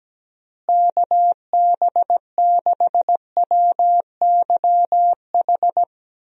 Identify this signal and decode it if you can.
Morse code, heard as a keyed tone, about 17 wpm, 700 Hz. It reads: KB6WYH
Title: USA Call Signs - 6 Characters - Repeated 3x - 17wpm